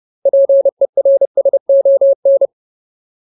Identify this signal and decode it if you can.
Morse code, heard as a keyed tone, about 30 wpm, 550 Hz. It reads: PERSON
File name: Ultimate Word List / 30WPM Ultimate